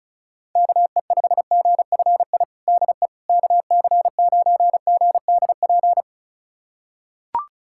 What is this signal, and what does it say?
Morse code, heard as a keyed tone, about 36 wpm, 700 Hz. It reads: KE5GFI DE KC9GDP
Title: USA Callsign DE Callsign 35wpm